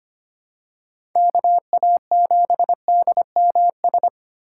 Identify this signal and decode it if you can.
Morse code, heard as a keyed tone, about 25 wpm, 700 Hz. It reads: KA7DMS
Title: USA Call Signs - 6 Characters - 2x Character Spacing - 25wpm